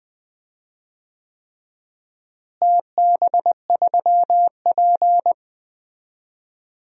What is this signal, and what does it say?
Morse code, heard as a keyed tone, about 20 wpm, 700 Hz. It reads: TB3P
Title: DXCC-100 Call Signs - 4 Characters - Repeated 3x - 20wpm